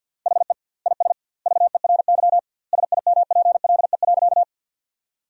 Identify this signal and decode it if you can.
Morse code, heard as a keyed tone, about 50 wpm, 700 Hz. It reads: HE IS VERY SIMPLE.
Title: 2x Word-Spacing - Sentences from Top 400 Words - 50wpm